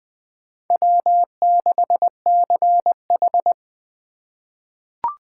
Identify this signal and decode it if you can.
Morse code, heard as a keyed tone, about 21 wpm, 700 Hz. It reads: W6CH
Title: USA Call Signs - 4x Character Spacing - 20wpm